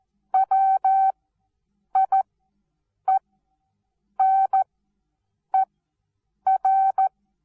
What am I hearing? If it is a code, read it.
Morse code, heard as a keyed tone, about 14 wpm, spaced as for 7 wpm, 770 Hz. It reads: WIENER
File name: Onlinelehrgang / morsen / DJ4UF Morsen19